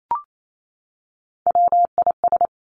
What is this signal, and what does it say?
Morse code, heard as a keyed tone, about 28 wpm, 700 Hz. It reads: WIS
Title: Three Challenging Letter-Number Combinations 28wpm